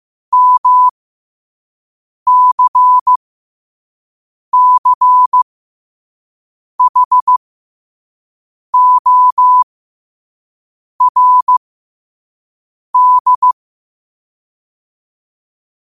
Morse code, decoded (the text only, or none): MCCHORD